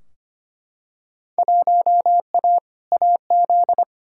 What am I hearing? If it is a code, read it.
Morse code, heard as a keyed tone, about 25 wpm, 700 Hz. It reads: 1A AZ
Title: Field Day Exchanges - 25wpm